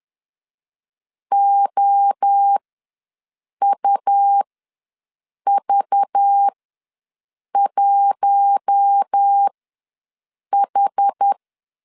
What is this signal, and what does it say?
Morse code, heard as a keyed tone, about 10 wpm, spaced as for 6 wpm, 800 Hz. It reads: OUV1H
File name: LEZIONI / 7 GF9Z / 7 G ws 10wpm 3843